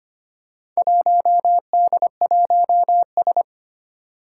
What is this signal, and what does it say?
Morse code, heard as a keyed tone, about 25 wpm, 700 Hz. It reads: 1D1S
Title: Four Challenging Letter-Number Combinations 25wpm